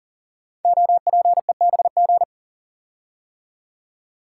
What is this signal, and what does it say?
Morse code, heard as a keyed tone, about 40 wpm, 700 Hz. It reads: OPEBG
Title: Code Groups - 5 Letter:Numbers - 40wpm